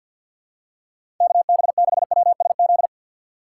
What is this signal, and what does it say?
Morse code, heard as a keyed tone, about 47 wpm, 700 Hz. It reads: KB6WSZ
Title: USA Call Signs - 6 Characters - 6x Character Spacing - 50wpm